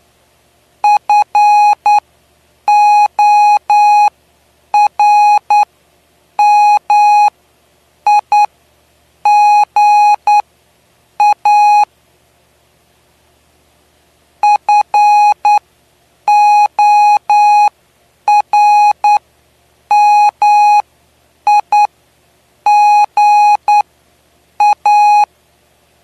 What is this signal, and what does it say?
Morse code, heard as a keyed tone, about 10 wpm, spaced as for 7 wpm, 830 Hz. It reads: FORMIGA FORMIGA
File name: cw / py2lcd / 09-PY2LCD-Li��o 9